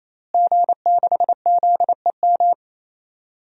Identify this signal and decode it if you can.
Morse code, heard as a keyed tone, about 27 wpm, 700 Hz. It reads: G6ZEM
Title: DXCC-100 Call Signs - 5 Characters - Repeated 3x - 28wpm